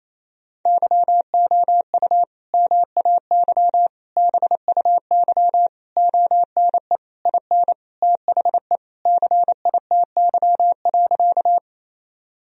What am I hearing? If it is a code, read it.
Morse code, heard as a keyed tone, about 28 wpm, 700 Hz. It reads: YOU MAY BUY ONE IN THE CITY.
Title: Sentences from Top 1000 Words - Review Part 4 - 28wpm